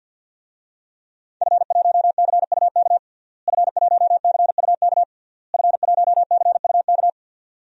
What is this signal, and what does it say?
Morse code, heard as a keyed tone, about 49 wpm, 700 Hz. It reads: F1CUK F1CUK F1CUK
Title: DXCC-100 Call Signs - 5 Characters - Repeated 3x - 50wpm